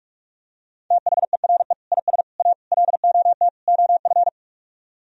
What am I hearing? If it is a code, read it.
Morse code, heard as a keyed tone, about 45 wpm, 700 Hz. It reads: THERE IS A LOT OF